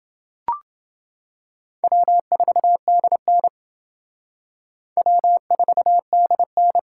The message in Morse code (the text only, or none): W4DN W4DN